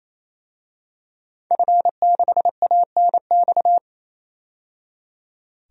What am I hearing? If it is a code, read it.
Morse code, heard as a keyed tone, about 28 wpm, 700 Hz. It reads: F6ANX
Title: DXCC-100 Call Signs - 5 Characters - 28wpm